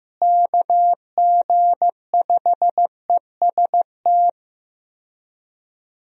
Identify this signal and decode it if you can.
Morse code, heard as a keyed tone, about 16 wpm, 700 Hz. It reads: KG5EST
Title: USA Call Signs - 6 Characters - Repeated 3x - 15wpm